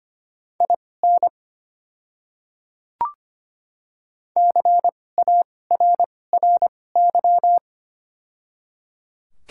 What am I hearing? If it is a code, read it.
Morse code, heard as a keyed tone, about 25 wpm, spaced as for 18 wpm, 700 Hz. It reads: IN CARRY
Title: 2x Character Spacing - Top 300 Words - 25wpm